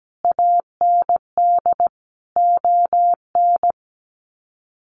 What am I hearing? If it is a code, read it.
Morse code, heard as a keyed tone, about 17 wpm, 700 Hz. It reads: AND ON
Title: Sets of 5 Words Encore2 17wpm